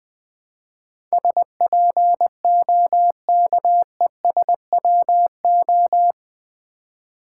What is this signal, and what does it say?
Morse code, heard as a keyed tone, about 20 wpm, 700 Hz. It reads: SPOKESWO
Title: Build A Word - 4 Letters At A Time - 11-Letter Words - 20wpm